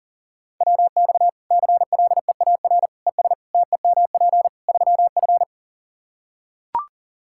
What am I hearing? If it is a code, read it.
Morse code, heard as a keyed tone, about 37 wpm, 700 Hz. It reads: WX CLEAR ES TEMP 3F
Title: Practice QSO - Exchange 2 - Info - 40wpm